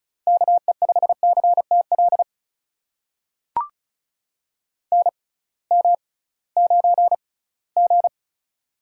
Morse code, heard as a keyed tone, about 36 wpm, spaced as for 20 wpm, 700 Hz. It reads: KE5CTL NM9G
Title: USA Call Signs - 6x Character Spacing - 35wpm